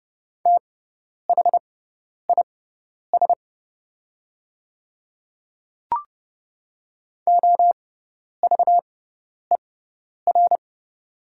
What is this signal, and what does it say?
Morse code, heard as a keyed tone, about 30 wpm, spaced as for 10 wpm, 700 Hz. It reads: THIS OVER
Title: 6x Character Spacing - Top 100 Words - 30wpm